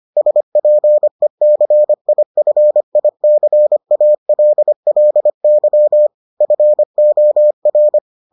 Morse code, heard as a keyed tone, about 25 wpm, 600 Hz. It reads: SPECIFICALLY FOR